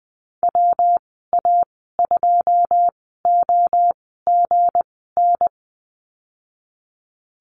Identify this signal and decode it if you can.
Morse code, heard as a keyed tone, about 20 wpm, spaced as for 14 wpm, 700 Hz. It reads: WA2OGN
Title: 213 Lesson 34g - Number 8 - Call Signs 20wpm